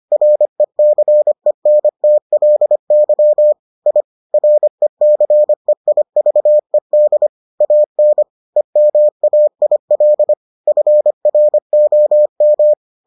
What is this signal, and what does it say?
Morse code, heard as a keyed tone, about 25 wpm, 600 Hz. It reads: RECENTLY I RECEIVED AN EMAIL FROM